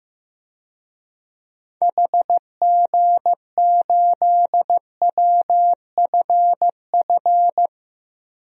Morse code, heard as a keyed tone, about 15 wpm, 700 Hz. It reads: HG8WFF
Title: DXCC-100 Call Signs - 6 Characters - 15wpm